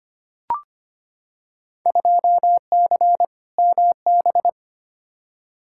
Morse code, heard as a keyed tone, about 25 wpm, 700 Hz. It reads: 2C MB
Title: Field Day Exchanges Encore - 25wpm